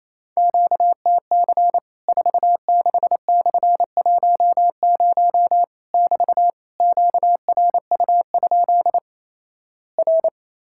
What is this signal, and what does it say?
Morse code, heard as a keyed tone, about 28 wpm, 700 Hz. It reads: QTC 46/10 <BT> QRU? R
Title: Worked All Europe DX Contest - QTC Exchange - 28wpm